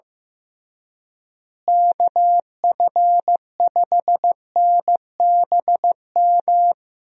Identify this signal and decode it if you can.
Morse code, heard as a keyed tone, about 15 wpm, 700 Hz. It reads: KF5NBM
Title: USA Call Signs - 6 Characters - Repeated 2x - 15wpm